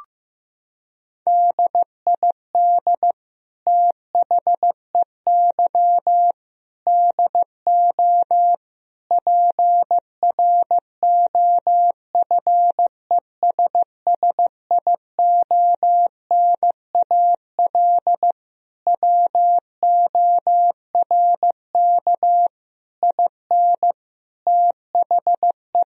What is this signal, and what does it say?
Morse code, heard as a keyed tone, about 15 wpm, 700 Hz. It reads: DID THEY DO PROFESSIONAL WORK IN THE
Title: Sentences from Top 1000 Words - Review Part 4 - 15wpm